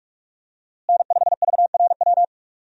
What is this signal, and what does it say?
Morse code, heard as a keyed tone, about 46 wpm, 700 Hz. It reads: N5VRW
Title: USA Call Signs - 5 Characters - Repeated 3x - 45wpm